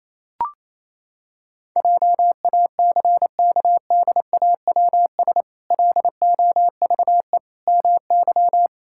Morse code, heard as a keyed tone, about 28 wpm, 700 Hz. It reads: JACKDAWS LOVE MY
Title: Pangram Challenge v2 - 28wpm